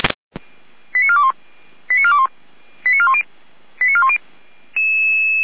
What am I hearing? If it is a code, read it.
Five-tone selective call (ZVEI), heard as quick runs of five tones, tones of about 70 ms. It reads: 89421 89421 89410 89410
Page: /455/SDR/ZVEI/